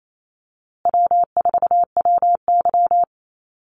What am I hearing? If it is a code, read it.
Morse code, heard as a keyed tone, about 28 wpm, 700 Hz. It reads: W4WY
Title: USA Call Signs Repeated 3x 28wpm